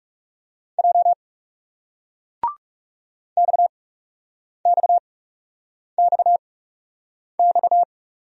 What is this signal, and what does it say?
Morse code, heard as a keyed tone, about 41 wpm, spaced as for 8 wpm, 700 Hz. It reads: J XXXX